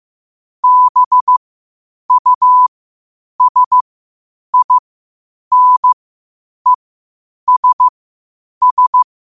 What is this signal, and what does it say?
Morse code, heard as a keyed tone, about 15 wpm, spaced as for 8 wpm, 1000 Hz. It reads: BUSINESS
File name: 8wpm